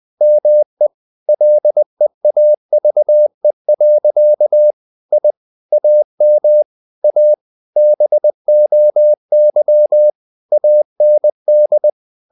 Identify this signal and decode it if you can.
Morse code, heard as a keyed tone, about 20 wpm, 600 Hz. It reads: ME LEAVE. I AM A BOY AND
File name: the old man and the sea / the old man and the sea part 1 20wpm